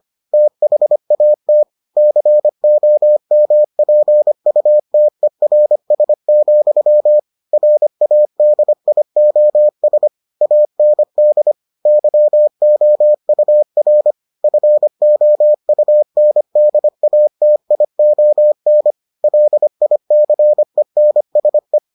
Morse code, heard as a keyed tone, about 25 wpm, 600 Hz. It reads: THAT COMPUTERS, RADIOS AND YOUR FOUNDATION LICENSE